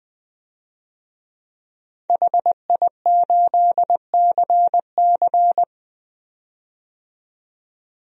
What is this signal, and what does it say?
Morse code, heard as a keyed tone, about 20 wpm, 700 Hz. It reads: HI8CC
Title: DXCC-100 Call Signs - 5 Characters - 20wpm